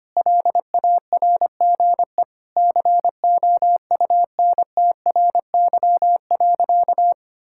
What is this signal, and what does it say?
Morse code, heard as a keyed tone, about 25 wpm, 700 Hz. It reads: LARGE COUNTRY.
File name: practice / 25wpm / Sentences with 4-Word Phrases - 25wpm